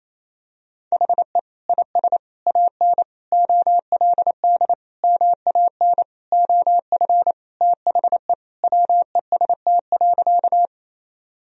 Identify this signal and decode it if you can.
Morse code, heard as a keyed tone, about 28 wpm, 700 Hz. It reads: HE IS AN OLD MAN OF THE WEST.
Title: Sentences from Top 1000 Words - Review Part 2 - 28wpm